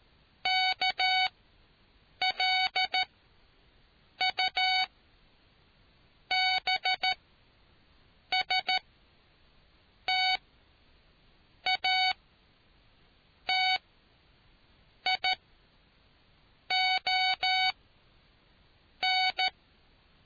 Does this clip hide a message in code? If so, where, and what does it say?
Yes: Morse code from 0.5 to 19.5 s, heard as a keyed tone, about 13 wpm, spaced as for 6 wpm, 750 Hz. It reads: KLUBSTATION